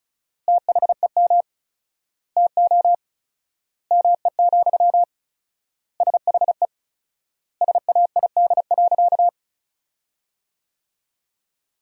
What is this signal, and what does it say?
Morse code, heard as a keyed tone, about 34 wpm, 700 Hz. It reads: THEM TO ME, SHE SAID.
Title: 4x Word-Spacing - Sentences from Top 500 Words - 35wpm